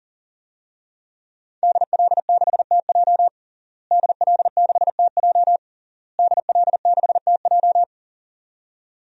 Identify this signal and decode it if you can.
Morse code, heard as a keyed tone, about 40 wpm, 700 Hz. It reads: DL6TJ DL6TJ DL6TJ
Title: DXCC-100 Call Signs - 5 Characters - Repeated 3x - 40wpm